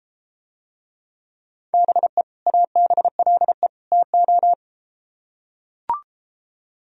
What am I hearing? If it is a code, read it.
Morse code, heard as a keyed tone, about 33 wpm, 700 Hz. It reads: BE ABLE TO